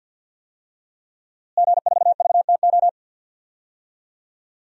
Morse code, heard as a keyed tone, about 52 wpm, 700 Hz. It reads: G4VTO
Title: DXCC-100 Call Signs - 5 Characters - Repeated 3x - 50wpm